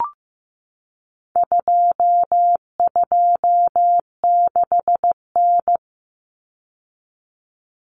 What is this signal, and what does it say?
Morse code, heard as a keyed tone, about 15 wpm, 700 Hz. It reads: 226N
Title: Common Cut Numbers - 4-Digit 15wpm